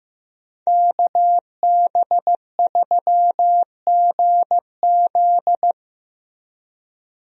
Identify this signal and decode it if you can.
Morse code, heard as a keyed tone, about 15 wpm, 700 Hz. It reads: KB3GZ
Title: USA Call Signs - 5 Characters - Repeated 2x - 15wpm